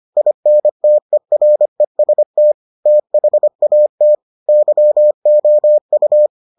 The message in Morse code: INTEREST THAT YOU

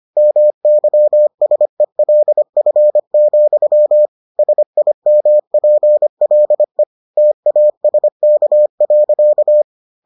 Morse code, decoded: MYSELF, SIMPLE TASK.

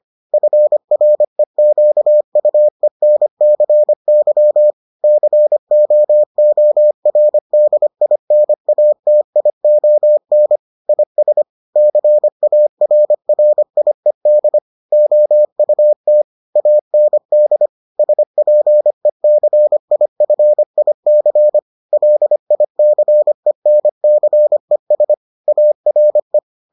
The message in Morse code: FREQUENCY COORDINATION IS CARRIED OUT AND SPECIFIC LICENCES ARE

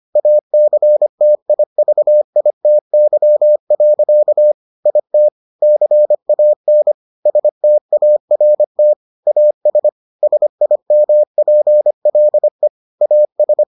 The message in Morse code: ACTIVITY. IT CAN START AS SIMPLE AS

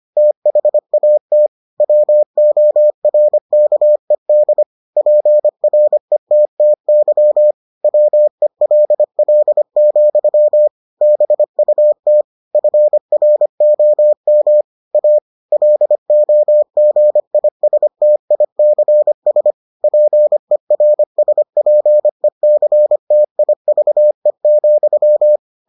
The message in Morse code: THAT WORKED PRETTY WELL, BUT FROM A LOGISTICS PERSPECTIVE,

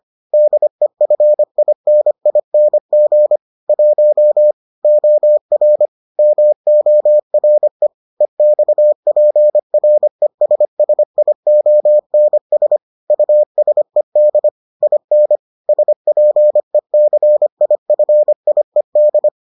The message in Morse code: DEFINING 1 OR MORE EXPRESSIONS USED IN SPECIFIED

